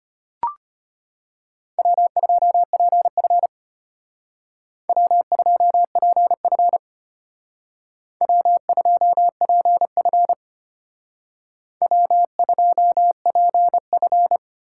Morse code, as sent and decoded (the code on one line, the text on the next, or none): .-- ..--- .--. ..-. / .-- ..--- .--. ..-. / .-- ..--- .--. ..-. / .-- ..--- .--. ..-.
W2PF W2PF W2PF W2PF